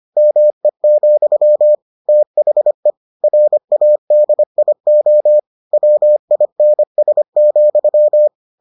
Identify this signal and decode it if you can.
Morse code, heard as a keyed tone, about 25 wpm, 600 Hz. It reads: ME, THE RADIO WINS,